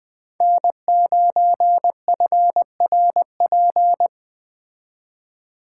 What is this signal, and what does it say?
Morse code, heard as a keyed tone, about 20 wpm, 700 Hz. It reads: N9FRP
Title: Speed Racing - USA 1x3 Call Signs 17wpm